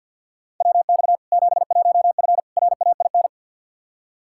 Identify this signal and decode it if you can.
Morse code, heard as a keyed tone, about 47 wpm, 700 Hz. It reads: WX 71F RAIN
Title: K1USN SST QSO Practice 50wpm